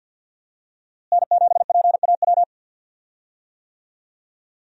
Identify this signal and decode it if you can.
Morse code, heard as a keyed tone, about 46 wpm, 700 Hz. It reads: N7PAW
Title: USA Call Signs - 5 Characters - Repeated 3x - 50wpm